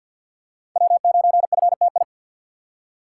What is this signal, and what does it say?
Morse code, heard as a keyed tone, about 52 wpm, 700 Hz. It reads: W9FTI
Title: USA Call Signs - 5 Characters - 3x Character Spacing - 50wpm